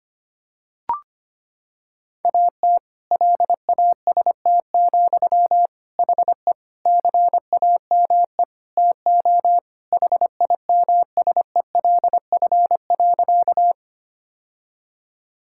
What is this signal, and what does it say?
Morse code, heard as a keyed tone, about 25 wpm, 700 Hz. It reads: AT LAST, HE CAME TO HIMSELF.
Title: Sentences from Top 1600 Words 25wpm